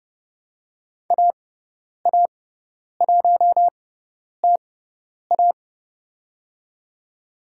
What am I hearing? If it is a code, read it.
Morse code, heard as a keyed tone, about 30 wpm, spaced as for 10 wpm, 690 Hz. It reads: AA1TA
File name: Lesson 18g - Letter m - Call Signs 30wpm